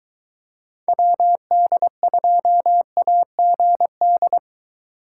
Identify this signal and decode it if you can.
Morse code, heard as a keyed tone, about 22 wpm, 700 Hz. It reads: WD2AGD